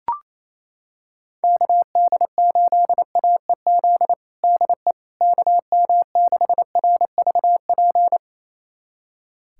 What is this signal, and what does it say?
Morse code, heard as a keyed tone, about 28 wpm, 700 Hz. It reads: KD8AEZ DE KM6RVP
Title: USA Callsign DE Callsign - 6 Characters - 28wpm